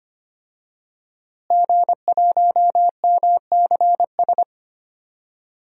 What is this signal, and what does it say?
Morse code, heard as a keyed tone, about 25 wpm, 700 Hz. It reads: G1MCS
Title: DXCC-100 Call Signs - 5 Characters - Repeated 3x - 25wpm